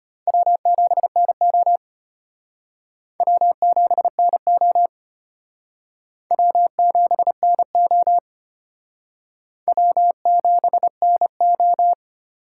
Morse code, heard as a keyed tone, about 33 wpm, 700 Hz. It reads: W7NO W7NO W7NO W7NO